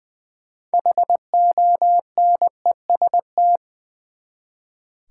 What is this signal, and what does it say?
Morse code, heard as a keyed tone, about 20 wpm, 700 Hz. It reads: HONEST
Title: Six Letter Words - 20wpm